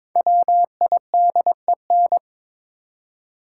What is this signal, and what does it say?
Morse code, heard as a keyed tone, about 22 wpm, 700 Hz. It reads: WIDEN
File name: 5 Letter Words - First 3 Letters Given - 22wpm